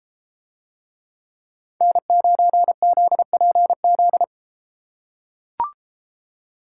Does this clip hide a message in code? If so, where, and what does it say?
Yes: Morse code from 1.8 to 4.2 s, heard as a keyed tone, about 32 wpm, 700 Hz. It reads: N9ZPZ